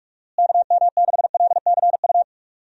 Morse code, heard as a keyed tone, about 47 wpm, 700 Hz. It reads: KM6LCU
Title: USA Call Signs - 6 Characters - Repeated 3x - 45wpm